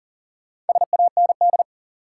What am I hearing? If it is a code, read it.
Morse code, heard as a keyed tone, about 40 wpm, 700 Hz. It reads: SAND
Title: Four Letter Words - 40wpm